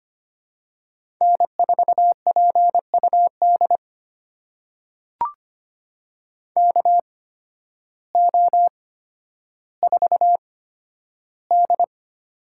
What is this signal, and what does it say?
Morse code, heard as a keyed tone, about 25 wpm, spaced as for 12 wpm, 700 Hz. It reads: N4PUD KO4D
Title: USA Call Signs - 5 Characters - 8x Character Spacing - 25wpm